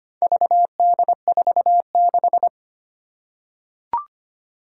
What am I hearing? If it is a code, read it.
Morse code, heard as a keyed tone, about 25 wpm, 700 Hz. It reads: VD46
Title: Four Challenging Letter-Number Combinations 25wpm